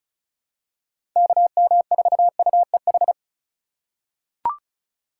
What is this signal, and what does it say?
Morse code, heard as a keyed tone, about 34 wpm, 700 Hz. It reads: KM4UEH
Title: USA Call Signs - 6 Characters - 4x Character Spacing - 35wpm